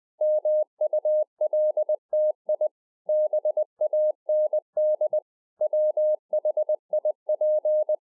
Morse code, heard as a keyed tone, about 20 wpm, 610 Hz. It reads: MULTI BAND WHIP